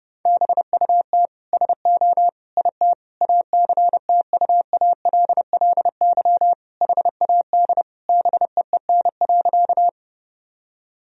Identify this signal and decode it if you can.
Morse code, heard as a keyed tone, about 30 wpm, 700 Hz. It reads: BUT SO IT ACTUALLY HAD BEEN.
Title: Sentences from Top 1000 Words - Review Part 4 - 30wpm